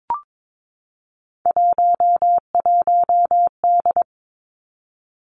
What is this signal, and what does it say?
Morse code, heard as a keyed tone, about 22 wpm, 700 Hz. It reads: 11D
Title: Three Challenging Letter-Number Combinations 22wpm